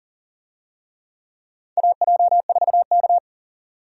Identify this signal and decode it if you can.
Morse code, heard as a keyed tone, about 40 wpm, 700 Hz. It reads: AJ4K
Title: USA Call Signs - 4 Characters - 40wpm